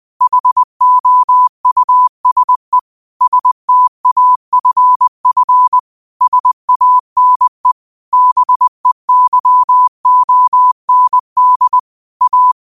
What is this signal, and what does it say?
Morse code, heard as a keyed tone, about 20 wpm, 1000 Hz. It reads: HOUSE STAFF SANE BEYOND A
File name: qotdfiles / 20wpm